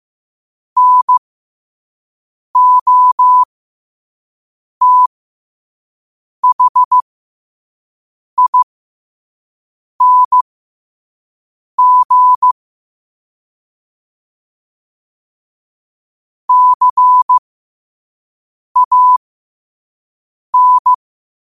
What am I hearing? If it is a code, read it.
Morse code, heard as a keyed tone, about 15 wpm, spaced as for 5 wpm, 1000 Hz. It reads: NOTHING CAN
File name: qotdfiles / 5wpm